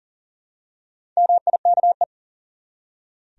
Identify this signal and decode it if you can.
Morse code, heard as a keyed tone, about 40 wpm, 700 Hz. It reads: MIKE
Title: USA Common Names - Max 5 Letters - 40wpm